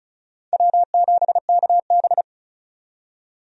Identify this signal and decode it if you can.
Morse code, heard as a keyed tone, about 34 wpm, 700 Hz. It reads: W7KB
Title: USA Call Signs - 4 Characters - 35wpm